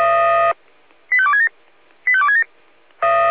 Five-tone selective call (ZVEI), heard as quick runs of five tones, tones of about 70 ms. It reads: 85267 85267